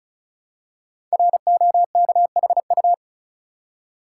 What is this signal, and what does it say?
Morse code, heard as a keyed tone, about 36 wpm, 700 Hz. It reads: ROKHU
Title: Code Groups - 5 Letters - 35wpm